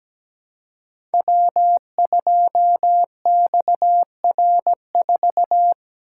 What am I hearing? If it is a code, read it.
Morse code, heard as a keyed tone, about 17 wpm, 700 Hz. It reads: W2XR4
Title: Code Groups - 5 Letter:Numbers - 17wpm